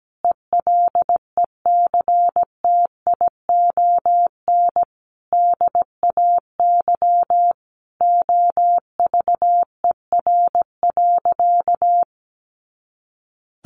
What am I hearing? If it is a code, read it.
Morse code, heard as a keyed tone, about 17 wpm, 700 Hz. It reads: ELECTION DAY OVER.
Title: A Sound Of Thunder 17wpm